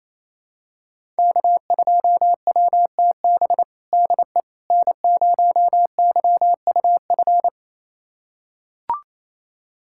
Morse code, heard as a keyed tone, about 27 wpm, 700 Hz. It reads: K2WTB DE N0YUF
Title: USA Callsign DE Callsign - 5 Characters - 28wpm